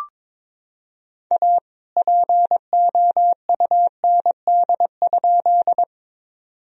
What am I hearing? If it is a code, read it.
Morse code, heard as a keyed tone, about 22 wpm, 700 Hz. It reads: A POUND?
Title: Sentences from Top 1000 Words - Review Part 2 - 22wpm